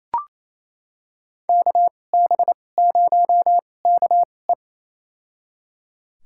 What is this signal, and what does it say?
Morse code, heard as a keyed tone, about 28 wpm, spaced as for 20 wpm, 700 Hz. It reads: KB0KE
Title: USA Call Signs - 5 Characters - 2x Character Spacing - 28wpm